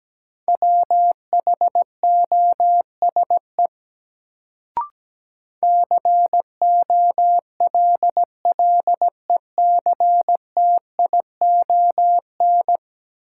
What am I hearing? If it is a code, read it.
Morse code, heard as a keyed tone, about 17 wpm, 700 Hz. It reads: WHOSE COLLECTION
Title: Sentences from Top 1200 Words - New Word Drill Alt - 17wpm